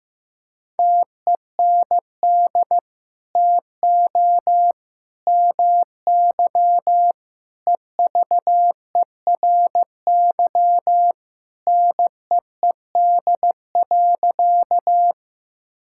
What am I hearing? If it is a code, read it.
Morse code, heard as a keyed tone, about 15 wpm, 700 Hz. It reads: TEND TO MY EVERY NEED.
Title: Sentences from Top 1000 Words - Review Part 5 - 15wpm